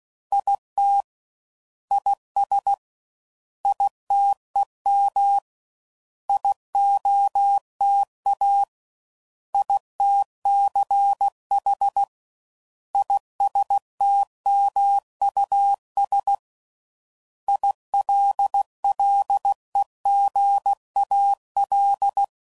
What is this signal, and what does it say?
Morse code, heard as a keyed tone, about 16 wpm, 800 Hz. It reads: IT IS ITEM IOTA ITCH ISTMUS ILLEGAL
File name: Prefixes / By First Letter / Is